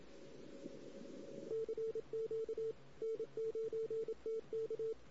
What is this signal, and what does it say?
Morse code, heard as a keyed tone, about 27 wpm, 440 Hz. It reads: CQ N9TK
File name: CW / 2008-ss-cw-k5tr-op-n5rz -16:00